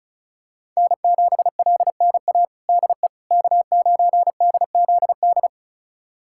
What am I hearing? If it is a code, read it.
Morse code, heard as a keyed tone, about 36 wpm, 700 Hz. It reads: N7LNA DE K9DZD
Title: USA Callsign DE Callsign - 5 Characters - 35wpm